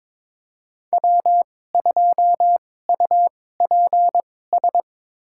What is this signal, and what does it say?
Morse code, heard as a keyed tone, about 22 wpm, spaced as for 16 wpm, 700 Hz. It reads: W2UPS
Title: USA Call Signs - 2x Character Spacing - 22wpm